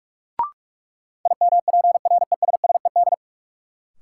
Morse code, heard as a keyed tone, about 46 wpm, 700 Hz. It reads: IMPRESSED
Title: Sentences from Top 1600 Words - New Word Drill Alt - 45wpm